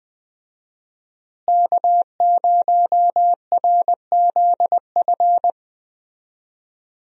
Morse code, heard as a keyed tone, about 20 wpm, 700 Hz. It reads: K0RZF
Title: USA Call Signs - 5 Characters - Repeated 2x - 20wpm